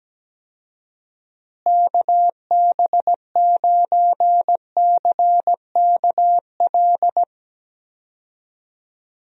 Morse code, heard as a keyed tone, about 17 wpm, 700 Hz. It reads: KB9CKL
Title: USA Call Signs - 6 Characters - Repeated 2x - 17wpm